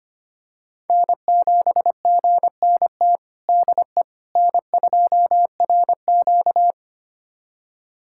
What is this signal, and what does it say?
Morse code, heard as a keyed tone, about 25 wpm, 700 Hz. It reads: N7GNT DE N2RQ
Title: USA Callsign DE Callsign 25wpm